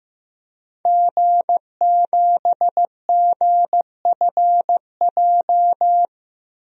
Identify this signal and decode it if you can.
Morse code, heard as a keyed tone, about 15 wpm, 700 Hz. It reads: G7GFJ
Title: DXCC-100 Call Signs - 5 Characters - Repeated 3x - 15wpm